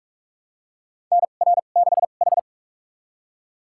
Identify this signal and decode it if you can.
Morse code, heard as a keyed tone, about 46 wpm, spaced as for 33 wpm, 700 Hz. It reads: NR6H
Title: USA Call Signs - 4 Characters - 2x Character Spacing - 45wpm